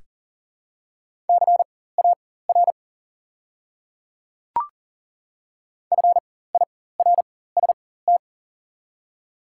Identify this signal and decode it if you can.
Morse code, heard as a keyed tone, about 38 wpm, spaced as for 18 wpm, 700 Hz. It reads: CAR FIRST
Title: 4x Character Spacing - Top 300 Words - 40wpm